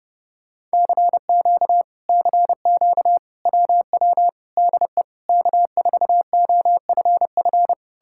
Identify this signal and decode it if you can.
Morse code, heard as a keyed tone, about 33 wpm, 700 Hz. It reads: CQ CQ WW DE K4OFF